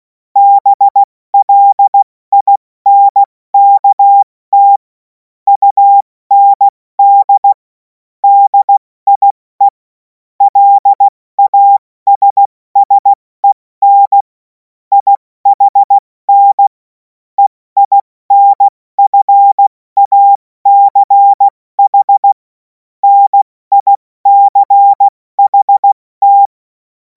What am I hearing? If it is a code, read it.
Morse code, heard as a keyed tone, about 16 wpm, spaced as for 14 wpm, 800 Hz. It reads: BLINKT UND DIE LASSEN IHN EINFACH NICHT
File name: Morsekurs / Witze / Witz 05. Mai 2026 70 F80